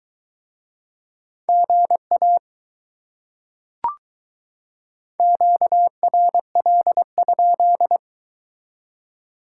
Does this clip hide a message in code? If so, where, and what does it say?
Yes: Morse code from 1.5 to 8.0 s, heard as a keyed tone, about 22 wpm, 700 Hz. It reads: GA QRL?